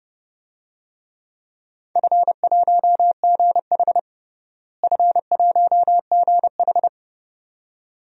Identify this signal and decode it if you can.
Morse code, heard as a keyed tone, about 32 wpm, 700 Hz. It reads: F1GH F1GH
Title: DXCC-100 Call Signs - 4 Characters - Repeated 2x - 30wpm